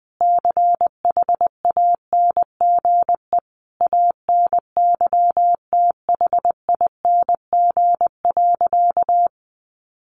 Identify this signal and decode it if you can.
Morse code, heard as a keyed tone, about 20 wpm, 700 Hz. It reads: CHANGE ANYTHING.
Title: A Sound Of Thunder 20wpm